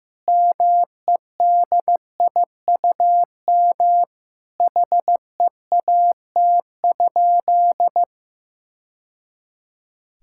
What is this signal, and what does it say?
Morse code, heard as a keyed tone, about 15 wpm, 700 Hz. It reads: MEDIUM HEAT?
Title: Sentences from Top 1000 Words - Review Part 6 - 15wpm